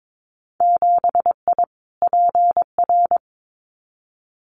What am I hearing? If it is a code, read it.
Morse code, heard as a keyed tone, about 22 wpm, 700 Hz. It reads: 7I PR
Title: Winter Field Day Exchanges Encore 22wpm